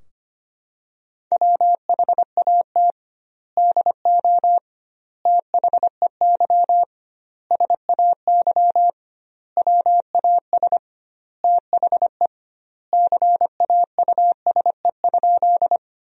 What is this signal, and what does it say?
Morse code, heard as a keyed tone, about 25 wpm, 700 Hz. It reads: WHAT DO THEY SAY WAS THE CAUSE?
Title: 2x Word-Spacing - Sentences from Top 200 Words - 25wpm